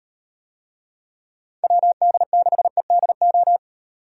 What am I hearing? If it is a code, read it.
Morse code, heard as a keyed tone, about 37 wpm, 700 Hz. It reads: WD6EDO